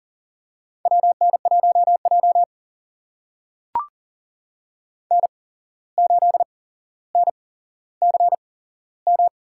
Morse code, heard as a keyed tone, about 40 wpm, spaced as for 16 wpm, 700 Hz. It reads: WN1J N8NCM
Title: USA Call Signs - 8x Character Spacing - 40wpm